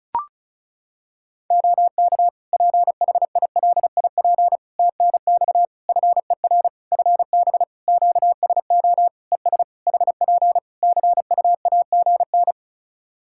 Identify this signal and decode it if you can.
Morse code, heard as a keyed tone, about 36 wpm, 700 Hz. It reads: OK PHILIP TNX FER FB QSO ES HP CUAGN
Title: K1USN SST QSO Practice 35wpm